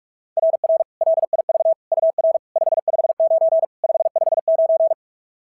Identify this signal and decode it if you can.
Morse code, heard as a keyed tone, about 44 wpm, 690 Hz. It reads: RR LIV UR 559 559
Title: SOTA QSO Practice 45wpm